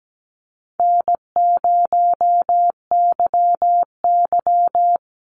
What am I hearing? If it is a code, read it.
Morse code, heard as a keyed tone, about 17 wpm, 700 Hz. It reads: N0YY
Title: Easy Call Signs Repeated 2x - 17wpm